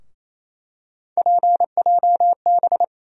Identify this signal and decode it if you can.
Morse code, heard as a keyed tone, about 28 wpm, 700 Hz. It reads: PJB
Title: USA Call Sign Postfixes - 3 Letters 28wpm